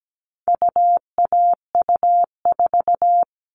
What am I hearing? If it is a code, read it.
Morse code, heard as a keyed tone, about 17 wpm, 700 Hz. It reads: UAU4
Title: All Cut Numbers - 4-Digit 17wpm